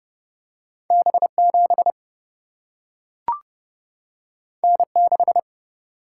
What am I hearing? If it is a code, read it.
Morse code, heard as a keyed tone, about 31 wpm, 700 Hz. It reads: B 7 N 6
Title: Single-Letter Single-Number 30wpm